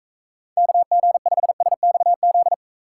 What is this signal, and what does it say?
Morse code, heard as a keyed tone, about 41 wpm, 700 Hz. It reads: KG5SXZ